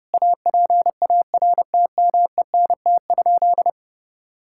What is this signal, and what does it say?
Morse code, heard as a keyed tone, about 32 wpm, 700 Hz. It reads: APARTMENT?
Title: Sentences from Top 1000 Words - Review Part 6 - 30wpm